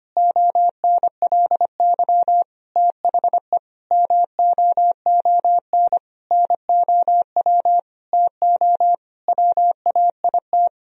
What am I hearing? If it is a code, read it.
Morse code, heard as a keyed tone, about 25 wpm, 700 Hz. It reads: ONLY THE MOON NOW TO WAIT
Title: Sentences from Top 1000 Words - Review Part 3 - 25wpm